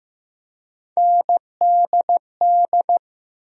Morse code, heard as a keyed tone, about 15 wpm, 700 Hz. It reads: NDD
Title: USA Call Sign Postfixes - 3 Letters 15wpm